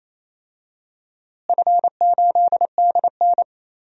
Morse code, heard as a keyed tone, about 28 wpm, 700 Hz. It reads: F8DN
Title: DXCC-100 Call Signs - 4 Characters - Repeated 2x - 28wpm